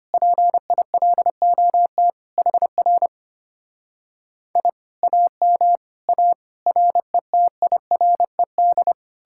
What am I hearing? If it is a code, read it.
Morse code, heard as a keyed tone, about 33 wpm, 700 Hz. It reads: PILOT HR I AM A RETIRED